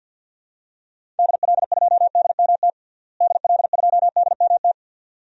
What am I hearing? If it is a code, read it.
Morse code, heard as a keyed tone, about 47 wpm, 700 Hz. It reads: DL2DMT DL2DMT
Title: DXCC-100 Call Signs - 6 Characters - Repeated 2x - 50wpm